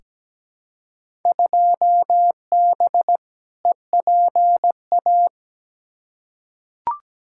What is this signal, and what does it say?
Morse code, heard as a keyed tone, about 17 wpm, 700 Hz. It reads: 2B EPA
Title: Field Day Exchanges Encore - 17wpm